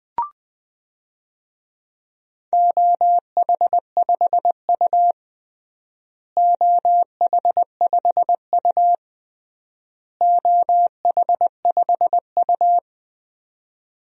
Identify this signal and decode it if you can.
Morse code, heard as a keyed tone, about 20 wpm, 700 Hz. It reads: OH5U OH5U OH5U
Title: DXCC-100 Call Signs - 4 Characters - Repeated 3x - 20wpm